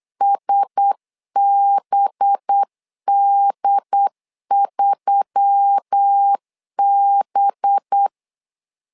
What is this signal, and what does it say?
Morse code, heard as a keyed tone, about 8 wpm, 800 Hz. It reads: SBD3B
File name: LEZIONI / 5 LBVC / 5 LB 8wpm 2816